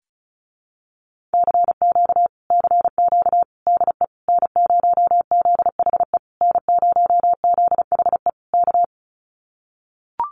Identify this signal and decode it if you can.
Morse code, heard as a keyed tone, about 34 wpm, 700 Hz. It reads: CQ CQ DE N0ZHE N0ZHE K
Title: Common QSO Phrases Encore 35wpm